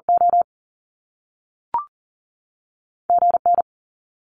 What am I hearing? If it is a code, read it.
Morse code, heard as a keyed tone, about 42 wpm, 700 Hz. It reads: O G N
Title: Top Two Letter Combinations in Words 40wpm